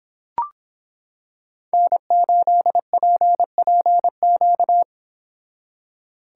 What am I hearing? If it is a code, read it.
Morse code, heard as a keyed tone, about 26 wpm, 700 Hz. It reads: N8PPQ